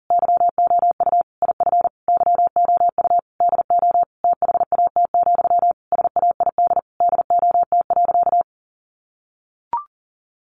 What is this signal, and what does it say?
Morse code, heard as a keyed tone, about 40 wpm, 700 Hz. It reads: YOU IF YOU DO THAT, SAID DOT.
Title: Sentences from Top 1500 Words 40wpm